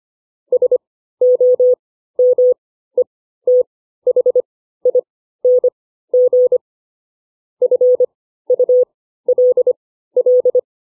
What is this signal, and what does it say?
Morse code, heard as a keyed tone, about 25 wpm, spaced as for 14 wpm, 500 Hz. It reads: SOMETHING FULL
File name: CW / farnsworth timing